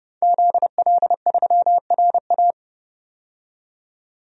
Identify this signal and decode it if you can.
Morse code, heard as a keyed tone, about 31 wpm, 700 Hz. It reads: ZL3RA
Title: DXCC-100 Call Signs - 5 Characters - 30wpm